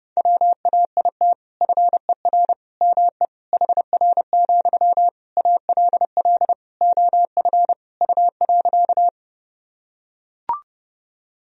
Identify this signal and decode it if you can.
Morse code, heard as a keyed tone, about 30 wpm, 700 Hz. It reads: WAIT FER ME HR, ALL OF U.
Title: Sentences from Top 500 Words and Abbreviations 30wpm